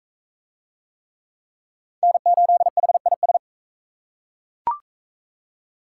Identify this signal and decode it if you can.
Morse code, heard as a keyed tone, about 41 wpm, 700 Hz. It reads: N8HIS